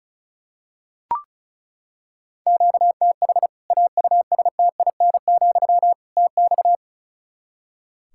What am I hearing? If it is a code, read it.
Morse code, heard as a keyed tone, about 36 wpm, 700 Hz. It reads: QTH AUSTIN, TX
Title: Practice QSO - Exchange 1 - Beginning - 35wpm